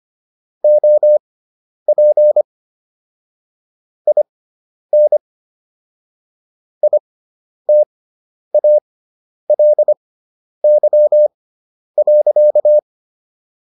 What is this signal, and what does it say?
Morse code, heard as a keyed tone, about 25 wpm, spaced as for 10 wpm, 600 Hz. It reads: OP IN ITALY.